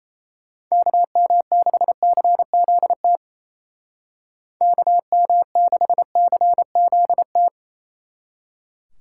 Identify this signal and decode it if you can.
Morse code, heard as a keyed tone, about 36 wpm, 700 Hz. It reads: KM6CZT KM6CZT